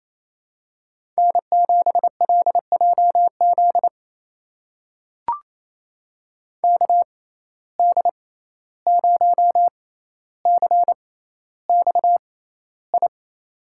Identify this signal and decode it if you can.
Morse code, heard as a keyed tone, about 28 wpm, spaced as for 14 wpm, 700 Hz. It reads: N7LJZ KD0CXI
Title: USA Call Signs - 6x Character Spacing - 28wpm